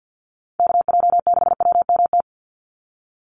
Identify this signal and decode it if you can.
Morse code, heard as a keyed tone, about 50 wpm, 700 Hz. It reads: KJ6WMT
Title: USA Call Signs 50wpm